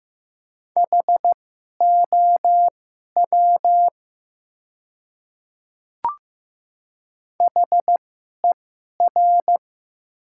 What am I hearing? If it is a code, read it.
Morse code, heard as a keyed tone, about 15 wpm, spaced as for 11 wpm, 700 Hz. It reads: HOW HER
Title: 2x Character Spacing - Top 100 Words - 15wpm